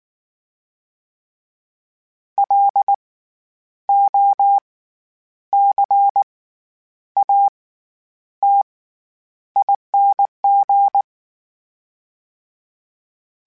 Morse code, heard as a keyed tone, about 19 wpm, spaced as for 9 wpm, 800 Hz. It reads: LOCATING
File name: Special INGs / Special INGs-mp3s / SlowToFaster ing @18